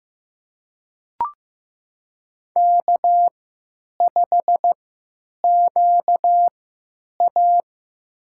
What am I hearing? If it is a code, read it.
Morse code, heard as a keyed tone, about 15 wpm, spaced as for 8 wpm, 700 Hz. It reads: K5QA
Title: USA Call Signs - 4 Characters - 3x Character Spacing - 15wpm